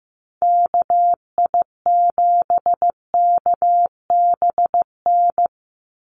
Morse code, heard as a keyed tone, about 15 wpm, 700 Hz. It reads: KI7KBN
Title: USA Call Signs 15wpm